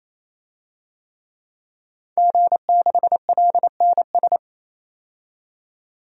Morse code, heard as a keyed tone, about 27 wpm, 700 Hz. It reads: G6LNS
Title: DXCC-100 Call Signs - 5 Characters - 28wpm